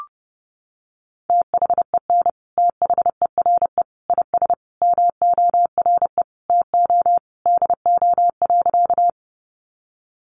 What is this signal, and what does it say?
Morse code, heard as a keyed tone, about 30 wpm, 700 Hz. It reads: THEN THERE IS MORE TO DO.
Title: Sentences from Top 100 Words 30wpm